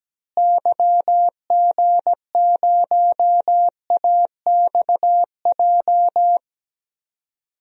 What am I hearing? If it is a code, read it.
Morse code, heard as a keyed tone, about 17 wpm, 700 Hz. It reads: YG0AXJ
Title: DXCC-100 Call Signs - 6 Characters - Repeated 3x - 17wpm